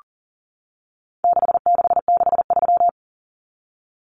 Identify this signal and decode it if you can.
Morse code, heard as a keyed tone, about 38 wpm, 700 Hz. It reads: 6663
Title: Numbers - 4-Digit 40wpm